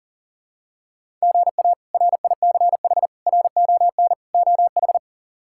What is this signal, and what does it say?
Morse code, heard as a keyed tone, about 40 wpm, 700 Hz. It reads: GA RICH RON OH
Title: K1USN SST QSO Practice - 40wpm